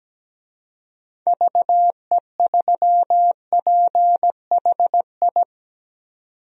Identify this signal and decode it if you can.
Morse code, heard as a keyed tone, about 17 wpm, 700 Hz. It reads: VE3PHI
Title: DXCC-100 Call Signs - 6 Characters - 17wpm